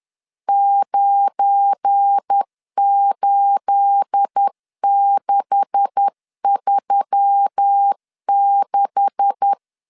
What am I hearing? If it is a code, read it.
Morse code, heard as a keyed tone, about 10 wpm, 800 Hz. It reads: 98636
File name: LEZIONI / 9 286 10wpm 2735